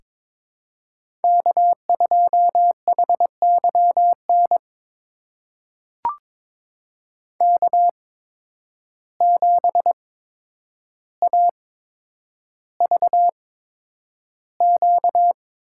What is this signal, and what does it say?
Morse code, heard as a keyed tone, about 22 wpm, spaced as for 9 wpm, 700 Hz. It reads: K2HYN K7AVQ
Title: USA Call Signs - 5 Characters - 8x Character Spacing - 22wpm